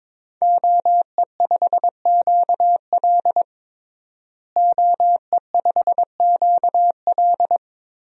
Morse code, heard as a keyed tone, about 22 wpm, 700 Hz. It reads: OE5QL OE5QL
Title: DXCC-100 Call Signs - 5 Characters - Repeated 2x - 22wpm